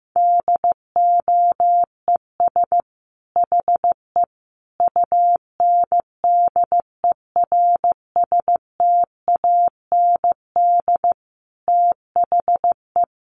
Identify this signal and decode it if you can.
Morse code, heard as a keyed tone, about 15 wpm, 700 Hz. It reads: DOES HE UNDERSTAND THE
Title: Sentences from Top 1100 Words 15wpm